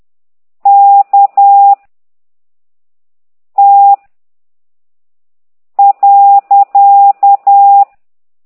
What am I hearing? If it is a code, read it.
Morse code, heard as a keyed tone, about 10 wpm, 800 Hz. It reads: K T .
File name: ct2idw / exerc9